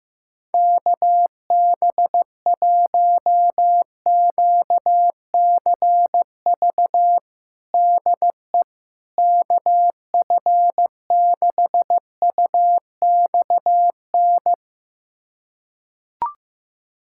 Morse code, heard as a keyed tone, about 15 wpm, 700 Hz. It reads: KB1QCV DE KF6UXN
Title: USA Callsign DE Callsign - 6 Characters - 15wpm